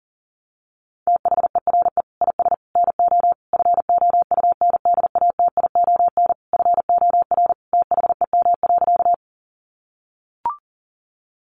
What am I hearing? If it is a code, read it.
Morse code, heard as a keyed tone, about 40 wpm, 700 Hz. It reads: THERE IS NO FOUNDATION FOR THEM.
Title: Sentences from Top 1200 Words 40wpm